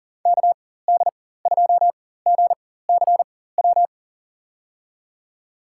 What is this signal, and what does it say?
Morse code, heard as a keyed tone, about 40 wpm, spaced as for 19 wpm, 700 Hz. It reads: KD2GCW
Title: USA Call Signs - 6 Characters - 4x Character Spacing - 40wpm